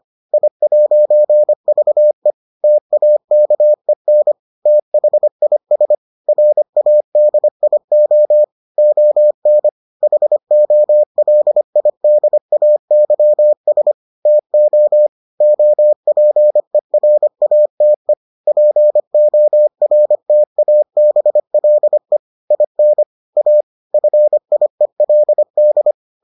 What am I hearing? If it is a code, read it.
Morse code, heard as a keyed tone, about 25 wpm, 600 Hz. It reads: I'VE TAKEN THIS RADIO ON HOLIDAYS TO OPERATE PORTABLE IN A FIELD